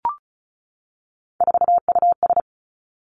Morse code, heard as a keyed tone, about 36 wpm, 700 Hz. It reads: 4US
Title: Three Challenging Letter-Number Combinations 35wpm